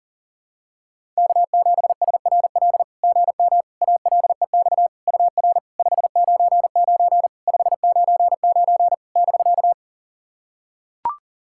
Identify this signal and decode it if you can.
Morse code, heard as a keyed tone, about 42 wpm, 690 Hz. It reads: K7SRL GM ALEX UR 599 599 <BK>
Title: SOTA QSO Practice 40wpm